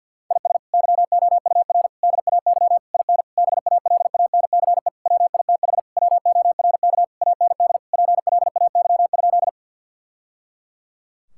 Morse code, heard as a keyed tone, about 50 wpm, 700 Hz. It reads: IS YOUR DAY IN BALANCE WITH WORK AND PLAY?
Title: Sentences from Top 1000 Words - Review Part 6 - 50wpm